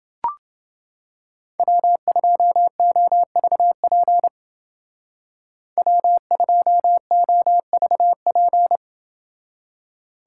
Morse code, heard as a keyed tone, about 28 wpm, 700 Hz. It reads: W2OVP W2OVP